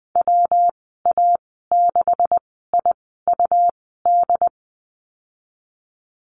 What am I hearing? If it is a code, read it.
Morse code, heard as a keyed tone, about 20 wpm, spaced as for 15 wpm, 700 Hz. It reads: WA6IUD
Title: 167 Lesson 27g - Number 7 - Call Signs 20wpm